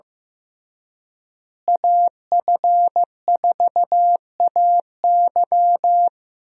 Morse code, heard as a keyed tone, about 15 wpm, 700 Hz. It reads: AF4AY
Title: USA Call Signs - 5 Characters - Repeated 3x - 15wpm